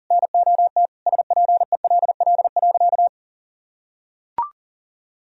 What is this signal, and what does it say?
Morse code, heard as a keyed tone, about 40 wpm, 700 Hz. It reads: NOT SPELL.
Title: Sentences from Top 1000 Words - Review Part 1 - 40wpm